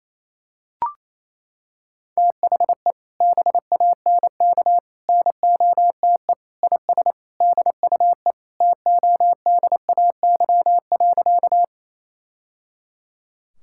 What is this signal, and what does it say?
Morse code, heard as a keyed tone, about 28 wpm, 700 Hz. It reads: THE BANK NOTE IS DUE TODAY.
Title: Sentences from Top 1000 Words - Review Part 6 - 28wpm